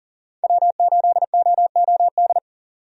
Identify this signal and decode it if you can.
Morse code, heard as a keyed tone, about 40 wpm, 700 Hz. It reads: W8OOD
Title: USA Call Signs - 5 Characters - 40wpm